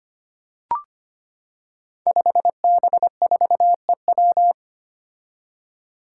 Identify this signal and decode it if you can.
Morse code, heard as a keyed tone, about 25 wpm, 700 Hz. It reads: 5B4EW
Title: Five Challenging Letter-Number Combinations 25wpm